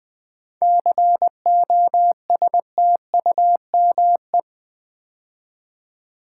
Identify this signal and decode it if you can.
Morse code, heard as a keyed tone, about 20 wpm, 700 Hz. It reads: COSTUME
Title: 7 Letter Words - First 3 Letters Given - 20wpm